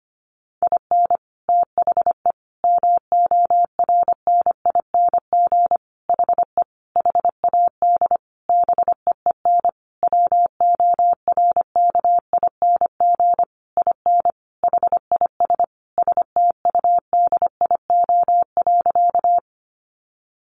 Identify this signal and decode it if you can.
Morse code, heard as a keyed tone, about 25 wpm, 700 Hz. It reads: IN THE MORNING HE HAD BEEN WORKING IN HIS STUDIO.
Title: Sentences from Top 1200 Words 25wpm